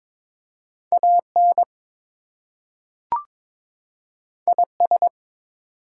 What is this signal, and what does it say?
Morse code, heard as a keyed tone, about 22 wpm, 700 Hz. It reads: AN IS
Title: Two Letter Words - 22wpm